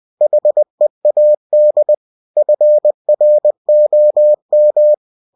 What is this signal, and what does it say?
Morse code, heard as a keyed tone, about 20 wpm, 600 Hz. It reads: HEAD FROM
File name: cw-books / winnie-the-pooh / winnie-20wpm-page-0026